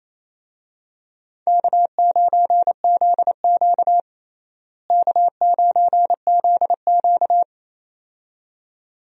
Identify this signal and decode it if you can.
Morse code, heard as a keyed tone, about 28 wpm, 700 Hz. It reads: K9ZQ K9ZQ
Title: USA Call Signs - 4 Characters - Repeated 2x - 28wpm